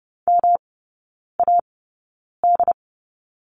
Morse code, heard as a keyed tone, about 30 wpm, spaced as for 9 wpm, 700 Hz. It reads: MAD
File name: Lesson 19e - Letter w - Three Characters 30wpm